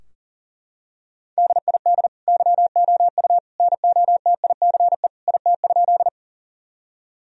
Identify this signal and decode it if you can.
Morse code, heard as a keyed tone, about 40 wpm, 700 Hz. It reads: DID YOU NOTICE IT?
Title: Sentences from Top 1000 Words - Review Part 2 - 40wpm